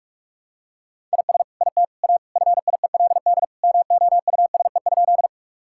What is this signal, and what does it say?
Morse code, heard as a keyed tone, about 41 wpm, 700 Hz. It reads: IS IT A FIELD MOUSE?
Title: Sentences from Top 1000 Words - Review Part 5 - 45wpm